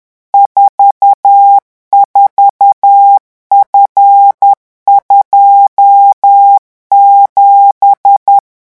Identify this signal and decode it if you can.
Morse code, heard as a keyed tone, about 10 wpm, 800 Hz. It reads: 44F27
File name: morse mp3 / revision13-vitesse 10 mots